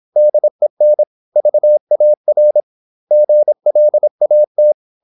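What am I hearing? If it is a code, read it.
Morse code, heard as a keyed tone, about 26 wpm, 600 Hz. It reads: DEN VAR GLAT
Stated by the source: High-speed (QRQ) Morse at 26 wpm.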